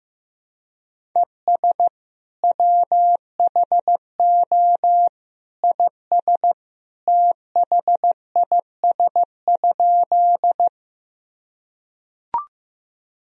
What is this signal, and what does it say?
Morse code, heard as a keyed tone, about 15 wpm, 700 Hz. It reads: ES WHO IS THIS?
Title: Sentences from Top 100 Words and Abbreviations 15wpm